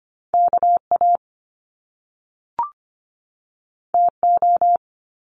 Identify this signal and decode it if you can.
Morse code, heard as a keyed tone, about 25 wpm, 700 Hz. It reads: K A T O
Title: Top Two Letter Combinations in Words 25wpm